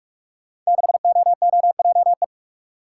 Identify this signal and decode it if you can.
Morse code, heard as a keyed tone, about 44 wpm, 700 Hz. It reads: 6OOJE
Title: Code Groups - 5 Letter:Numbers - 45wpm